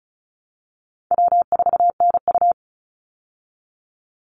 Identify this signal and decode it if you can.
Morse code, heard as a keyed tone, about 34 wpm, 700 Hz. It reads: W4NU
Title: Easy Call Signs Repeated 2x - 35wpm